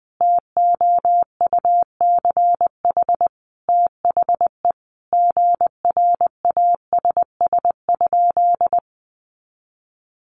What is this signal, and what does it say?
Morse code, heard as a keyed tone, about 20 wpm, 700 Hz. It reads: TOUCH THE GRASS?
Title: A Sound Of Thunder 20wpm